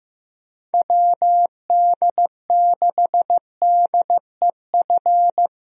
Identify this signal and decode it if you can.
Morse code, heard as a keyed tone, about 15 wpm, 700 Hz. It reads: WD6DEF
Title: USA Call Signs - 6 Characters - 15wpm